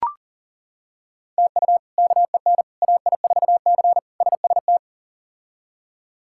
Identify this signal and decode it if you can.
Morse code, heard as a keyed tone, about 37 wpm, 690 Hz. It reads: TU KEN AI4C SST
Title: K1USN SST QSO Practice - 40wpm